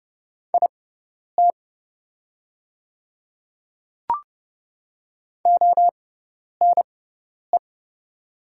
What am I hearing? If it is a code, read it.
Morse code, heard as a keyed tone, about 30 wpm, spaced as for 10 wpm, 700 Hz. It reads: IT ONE
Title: 6x Character Spacing - Top 200 Words - 30wpm